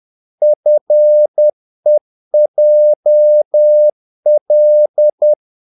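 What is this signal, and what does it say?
Morse code, heard as a keyed tone, about 10 wpm, 600 Hz. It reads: FEJL
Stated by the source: Morse practice of short Danish words at 10 wpm, from CW med Gnister.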